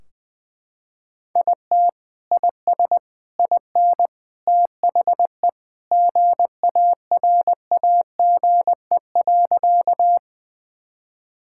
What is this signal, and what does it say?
Morse code, heard as a keyed tone, about 20 wpm, 700 Hz. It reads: IT IS IN THE GARAGE.
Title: Sentences from Top 1600 Words 20wpm